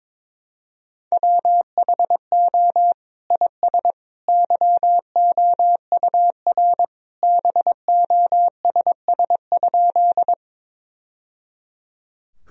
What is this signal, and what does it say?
Morse code, heard as a keyed tone, about 22 wpm, 700 Hz. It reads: WHO IS YOUR BOSS?
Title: Sentences from Top 1000 Words - Review Part 4 - 22wpm